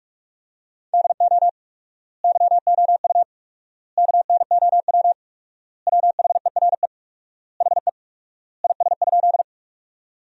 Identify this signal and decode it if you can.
Morse code, heard as a keyed tone, about 46 wpm, 700 Hz. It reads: DO YOU KNOW WHERE HE IS?
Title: 4x Word-Spacing - Sentences from Top 300 Words - 45wpm